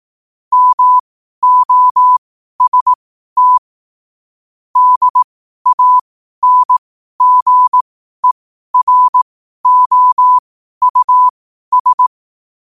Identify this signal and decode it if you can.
Morse code, heard as a keyed tone, about 18 wpm, spaced as for 12 wpm, 1000 Hz. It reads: MOST DANGEROUS
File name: qotdfiles / 12wpm